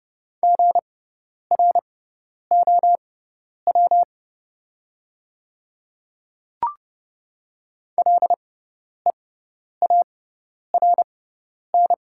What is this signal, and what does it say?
Morse code, heard as a keyed tone, about 30 wpm, spaced as for 10 wpm, 690 Hz. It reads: GROW LEARN
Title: 6x Character Spacing - Top 500 Words - 30wpm